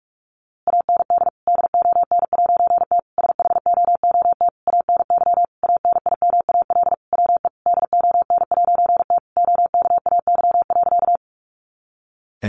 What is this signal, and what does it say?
Morse code, heard as a keyed tone, about 44 wpm, 700 Hz. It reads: AND DON'T SHOOT ANY ANIMAL WE DON'T OKAY.
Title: A Sound Of Thunder 45wpm